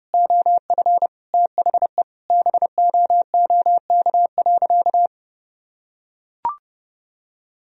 Morse code, heard as a keyed tone, about 30 wpm, 700 Hz. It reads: OF THE BOOK.
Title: Sentences from Top 1000 Words - Review Part 1 - 30wpm